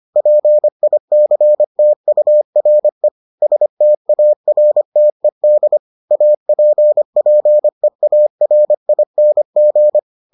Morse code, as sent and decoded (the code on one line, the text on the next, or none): .--. .. -.-. - ..- .-. / ... - .- .-. - . -.. / .- .--. .--. .- .-. .. -. --.
PICTURE STARTED APPEARING